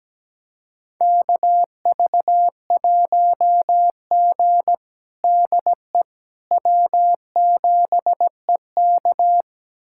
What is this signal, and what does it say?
Morse code, heard as a keyed tone, about 17 wpm, 700 Hz. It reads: KV1G DE W7EK
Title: USA Callsign DE Callsign - 4 Characters - 17wpm